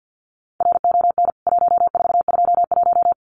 Morse code, heard as a keyed tone, about 49 wpm, 700 Hz. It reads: RON 1421
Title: CWT Exchange 50wpm